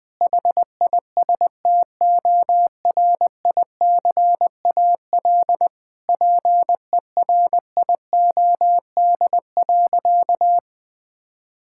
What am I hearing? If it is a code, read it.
Morse code, heard as a keyed tone, about 20 wpm, 700 Hz. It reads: HISTORICAL PERIOD.
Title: Sentences from Top 1000 Words - Review Part 5 - 20wpm